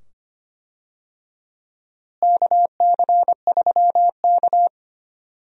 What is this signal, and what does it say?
Morse code, heard as a keyed tone, about 25 wpm, 700 Hz. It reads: KC3K
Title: USA Call Signs - 4 Characters - Repeated 2x - 25wpm